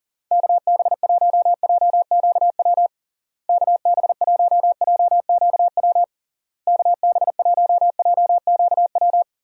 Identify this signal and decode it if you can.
Morse code, heard as a keyed tone, about 40 wpm, 700 Hz. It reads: KB1JQW KB1JQW KB1JQW
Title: USA Call Signs - 6 Characters - Repeated 3x - 40wpm